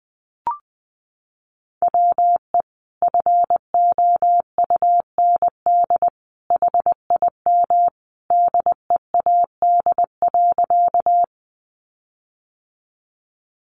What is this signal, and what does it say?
Morse code, heard as a keyed tone, about 19 wpm, 700 Hz. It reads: WE FOUND HIM DEAD.
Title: Sentences from Top 1500 Words 20wpm